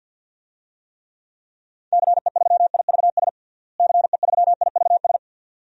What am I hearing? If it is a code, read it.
Morse code, heard as a keyed tone, about 51 wpm, 700 Hz. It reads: CE3IVS CE3IVS
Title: DXCC-100 Call Signs - 6 Characters - Repeated 2x - 50wpm